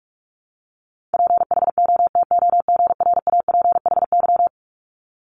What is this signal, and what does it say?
Morse code, heard as a keyed tone, about 46 wpm, 700 Hz. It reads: PHOTOGRAPHY
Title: Multi-letter Phonemes 45wpm